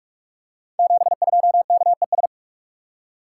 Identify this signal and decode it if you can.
Morse code, heard as a keyed tone, about 44 wpm, 690 Hz. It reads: 72KES
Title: Code Groups - 5 Letter:Numbers - 45wpm